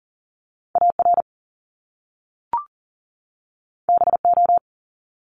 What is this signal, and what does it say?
Morse code, heard as a keyed tone, about 42 wpm, 700 Hz. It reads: AR BO
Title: Top Two Letter Combinations in Words 40wpm